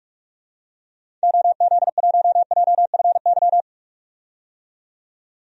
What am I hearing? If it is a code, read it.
Morse code, heard as a keyed tone, about 42 wpm, 700 Hz. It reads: OZ1JFY
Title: DXCC-100 Call Signs - 6 Characters - 45wpm